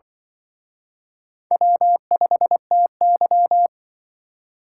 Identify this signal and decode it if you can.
Morse code, heard as a keyed tone, about 25 wpm, 700 Hz. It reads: W5TY
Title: Speed Racing - USA 1x2 Call Signs 20wpm